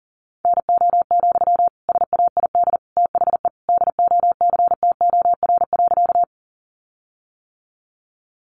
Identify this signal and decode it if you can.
Morse code, heard as a keyed tone, about 40 wpm, 700 Hz. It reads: NO, SAID THE DOCTOR.
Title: Sentences from Top 1500 Words 40wpm